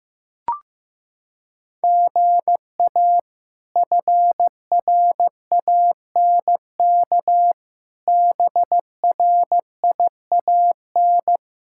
Morse code, heard as a keyed tone, about 15 wpm, 700 Hz. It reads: GA FRANK BRIAN
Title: K1USN SST QSO Practice - 15wpm